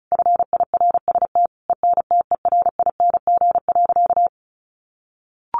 Morse code, heard as a keyed tone, about 34 wpm, 700 Hz. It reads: FIRST ENTERING.
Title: A Sound Of Thunder 35wpm